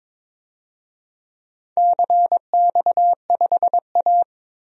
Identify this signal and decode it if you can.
Morse code, heard as a keyed tone, about 22 wpm, 700 Hz. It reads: CX5A
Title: DXCC-100 Call Signs - 4 Characters - 22wpm